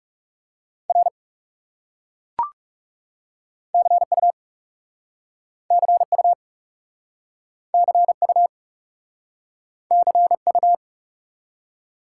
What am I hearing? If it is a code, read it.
Morse code, heard as a keyed tone, about 37 wpm, 700 Hz. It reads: R CU CU CU CU
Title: Speed-Racing Top 90 QSO Elements 30wpm